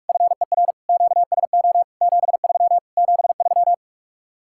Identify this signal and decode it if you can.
Morse code, heard as a keyed tone, about 45 wpm, 700 Hz. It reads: FER QSO 73 73